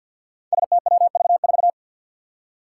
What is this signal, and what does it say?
Morse code, heard as a keyed tone, about 52 wpm, 700 Hz. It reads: STWV4
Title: Five Challenging Letter-Number Combinations 50wpm